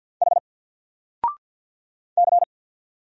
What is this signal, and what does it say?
Morse code, heard as a keyed tone, about 49 wpm, 700 Hz. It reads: H C
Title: Single Letter-Number 50wpm